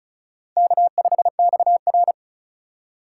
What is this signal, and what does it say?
Morse code, heard as a keyed tone, about 36 wpm, 700 Hz. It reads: K5XR
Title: USA Call Signs - 4 Characters - 4x Character Spacing - 35wpm